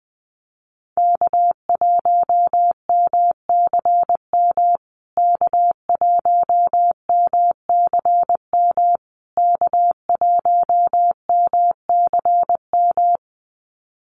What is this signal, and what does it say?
Morse code, heard as a keyed tone, about 19 wpm, 700 Hz. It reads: K1MCM K1MCM K1MCM
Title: USA Call Signs Repeated 3x 20wpm